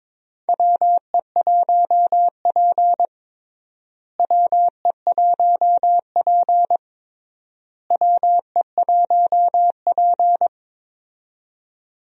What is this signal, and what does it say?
Morse code, heard as a keyed tone, about 22 wpm, 700 Hz. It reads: WE1P WE1P WE1P
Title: USA Call Signs - 4 Characters - Repeated 3x - 22wpm